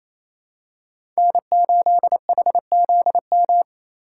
Morse code, heard as a keyed tone, about 28 wpm, 700 Hz. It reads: N8HZM
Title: K1USN SST QSO Practice - 28wpm